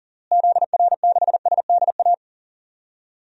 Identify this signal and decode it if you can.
Morse code, heard as a keyed tone, about 40 wpm, 700 Hz. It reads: ZR6SDA
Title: DXCC-100 Call Signs - 6 Characters - Repeated 3x - 40wpm